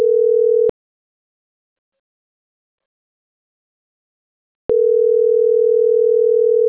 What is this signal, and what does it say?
Ringback tone (North America), 440 + 480 Hz.
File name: CENTRAL SOTOVA / out roberto.miranda